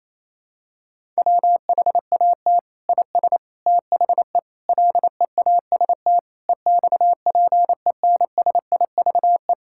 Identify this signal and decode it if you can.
Morse code, heard as a keyed tone, about 28 wpm, 700 Hz. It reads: WHAT IS THE LEAST EXPENSIVE
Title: Sentences from Top 1000 Words - Review Part 6 - 28wpm